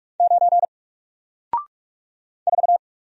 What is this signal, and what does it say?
Morse code, heard as a keyed tone, about 44 wpm, 700 Hz. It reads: 9 4
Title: Single Character (Letter:Number:Punctuation:Prosign) 45wpm